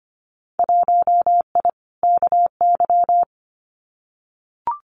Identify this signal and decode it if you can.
Morse code, heard as a keyed tone, about 25 wpm, 700 Hz. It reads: 1I KY
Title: Winter Field Day Exchanges 25wpm